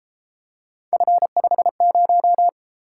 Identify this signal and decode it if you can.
Morse code, heard as a keyed tone, about 32 wpm, 700 Hz. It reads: F50